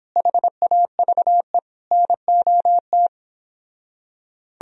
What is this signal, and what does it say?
Morse code, heard as a keyed tone, about 31 wpm, 700 Hz. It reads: HAVE NOT